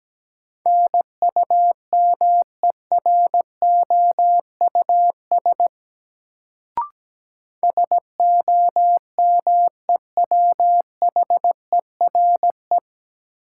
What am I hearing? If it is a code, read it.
Morse code, heard as a keyed tone, about 17 wpm, 700 Hz. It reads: NUMEROUS SOMEWHERE
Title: Sentences from Top 1400 Words - New Word Drill Alt - 17wpm